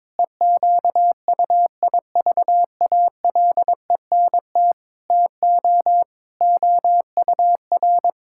Morse code, heard as a keyed tone, about 22 wpm, 700 Hz. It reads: EQUIVALENT TO OUR
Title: Sentences from Top 1600 Words 22wpm